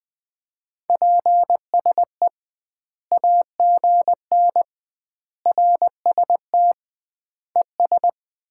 Morse code, heard as a keyed tone, about 20 wpm, 700 Hz. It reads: PSE AGN RST ES
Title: Common QSO Phrases - 2x Word-Spacing - 20wpm